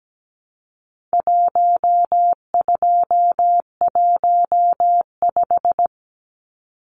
Morse code, heard as a keyed tone, about 17 wpm, 700 Hz. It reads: 1215
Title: Numbers - 4-Digit 17wpm